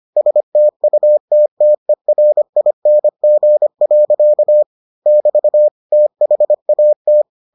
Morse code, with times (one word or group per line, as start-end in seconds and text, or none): 0.2-4.6 STUTTERING.
5.1-5.7 <BT>
5.9-7.2 THAT